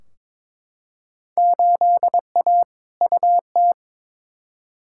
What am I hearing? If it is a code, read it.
Morse code, heard as a keyed tone, about 22 wpm, 700 Hz. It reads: 8A UT
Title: Field Day Exchanges Encore - 22wpm